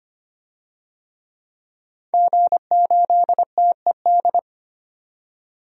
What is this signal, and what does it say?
Morse code, heard as a keyed tone, about 25 wpm, 700 Hz. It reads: G8TED
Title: DXCC-100 Call Signs - 5 Characters - 25wpm